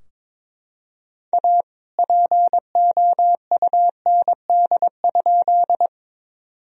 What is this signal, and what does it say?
Morse code, heard as a keyed tone, about 22 wpm, 700 Hz. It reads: A POUND?
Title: Sentences from Top 1000 Words - Review Part 2 - 22wpm